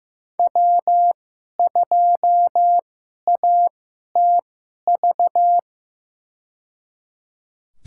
Morse code, heard as a keyed tone, about 15 wpm, spaced as for 11 wpm, 700 Hz. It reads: W2ATV
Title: USA Call Signs - 5 Characters - 2x Character Spacing - 15wpm